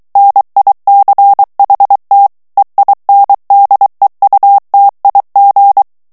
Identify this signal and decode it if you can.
Morse code, heard as a keyed tone, about 23 wpm, 800 Hz. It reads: NICHT EINDEUTIG
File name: morse mp3 / ALLEMAND texte clair 1 vitesse 22 mots minute